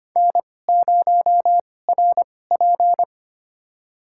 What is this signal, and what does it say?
Morse code, heard as a keyed tone, about 25 wpm, spaced as for 18 wpm, 700 Hz. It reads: N0RP
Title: USA Call Signs - 4 Characters - 2x Character Spacing - 25wpm